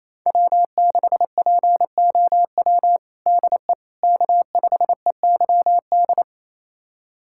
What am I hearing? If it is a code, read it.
Morse code, heard as a keyed tone, about 28 wpm, 700 Hz. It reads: W6POW DE K5EYD
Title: USA Callsign DE Callsign 28wpm